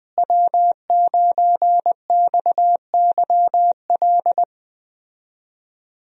Morse code, heard as a keyed tone, about 21 wpm, 700 Hz. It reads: W9XYL
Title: USA Call Signs - 5 Characters - Repeated 2x - 20wpm